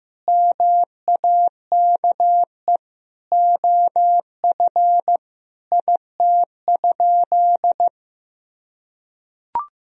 Morse code, heard as a keyed tone, about 15 wpm, 700 Hz. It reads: MAKE OF IT?
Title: Sentences from Top 100 Words and Abbreviations 15wpm